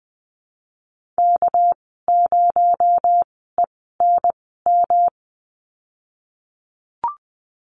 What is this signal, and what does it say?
Morse code, heard as a keyed tone, about 20 wpm, spaced as for 14 wpm, 700 Hz. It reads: K0ENM
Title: 245 Lesson 39g - Letter z - Call Signs 20wpm